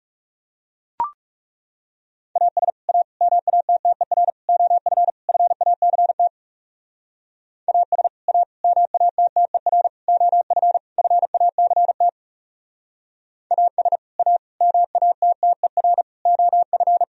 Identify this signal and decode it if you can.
Morse code, heard as a keyed tone, about 36 wpm, 700 Hz. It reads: AS A MATTER OF FACT AS A MATTER OF FACT AS A MATTER OF
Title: Speed-Racing - Sets of 5 Words 30wpm